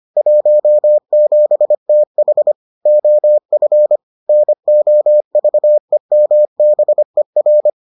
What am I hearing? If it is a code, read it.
Morse code, heard as a keyed tone, about 25 wpm, 600 Hz. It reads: 17TH OF NOVEMBER